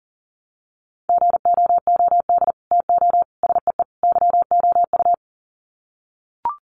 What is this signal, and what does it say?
Morse code, heard as a keyed tone, about 40 wpm, 700 Hz. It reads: GOOD TO SEE YOU
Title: Sets of 4 Words 40wpm